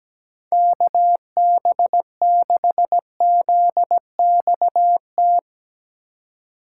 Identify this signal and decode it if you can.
Morse code, heard as a keyed tone, about 19 wpm, 700 Hz. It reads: KB6ZXT
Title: Speed Racing - USA 2x3 Call Signs 17wpm